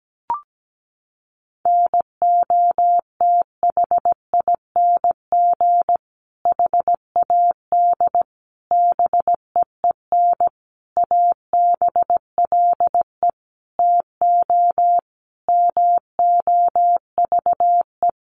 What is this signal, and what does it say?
Morse code, heard as a keyed tone, about 17 wpm, 700 Hz. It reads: NOTHING HAD BEEN ABLE TO MOVE
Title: Sentences from Top 1300 Words 17wpm